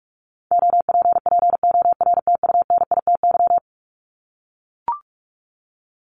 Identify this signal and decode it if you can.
Morse code, heard as a keyed tone, about 44 wpm, 700 Hz. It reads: OPPORTUNITY
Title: Top 751-1000 Words 45wpm